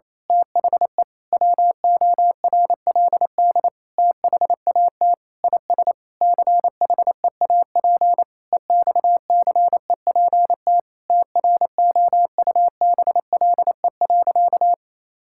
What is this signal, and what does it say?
Morse code, heard as a keyed tone, about 28 wpm, 700 Hz. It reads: THE WORLD THAT IS CHEAP EXCEPT TROUBLE.
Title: Sentences from Top 1600 Words 28wpm